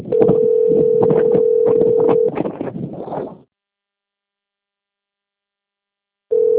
Ringback tone (North America), 440 + 480 Hz.
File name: rg-1600-unknown- - - .44715